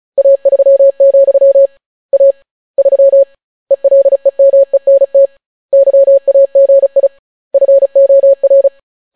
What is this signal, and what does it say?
Morse code, heard as a keyed tone, about 35 wpm, 550 Hz. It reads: A3, A 3 ELEMENT YAGI FOR